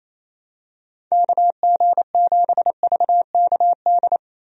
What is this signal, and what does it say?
Morse code, heard as a keyed tone, about 27 wpm, 700 Hz. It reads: KG7VKD
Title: USA Call Signs - 6 Characters - 28wpm